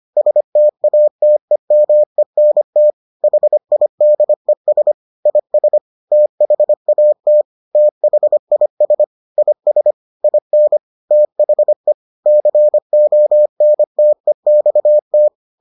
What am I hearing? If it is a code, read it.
Morse code, heard as a keyed tone, about 25 wpm, 600 Hz. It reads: STATEMENT HIDES IS THAT THIS IS IN THE CONTEXT